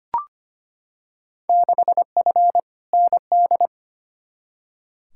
Morse code, heard as a keyed tone, about 25 wpm, 700 Hz. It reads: 6F ND
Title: Field Day Exchanges Encore - 25wpm